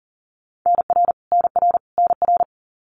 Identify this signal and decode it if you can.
Morse code, heard as a keyed tone, about 40 wpm, 700 Hz. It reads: NR NR NR
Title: Abbreviations Repeated 3x - 40wpm